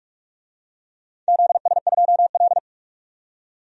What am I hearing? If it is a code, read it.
Morse code, heard as a keyed tone, about 44 wpm, 700 Hz. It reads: ZS2L
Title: DXCC-100 Call Signs - 4 Characters - 45wpm